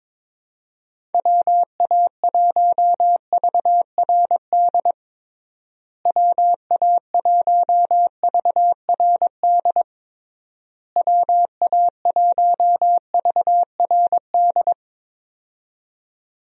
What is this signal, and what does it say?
Morse code, heard as a keyed tone, about 21 wpm, 700 Hz. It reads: WA1VRD WA1VRD WA1VRD
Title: USA Call Signs - 6 Characters - Repeated 3x - 22wpm